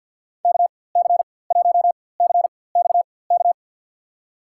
Morse code, heard as a keyed tone, about 48 wpm, spaced as for 23 wpm, 690 Hz. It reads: KC1CXK
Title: USA Call Signs - 4x Character Spacing - 50wpm